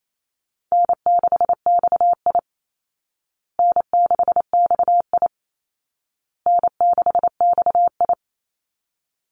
Morse code, heard as a keyed tone, about 28 wpm, 700 Hz. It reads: N6XI N6XI N6XI
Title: Call Signs Repeated 2x - 28wpm